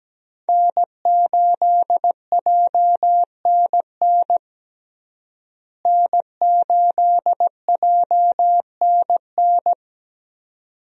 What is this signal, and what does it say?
Morse code, heard as a keyed tone, about 17 wpm, 700 Hz. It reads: N8JNN N8JNN
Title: USA Call Signs - 5 Characters - Repeated 2x - 17wpm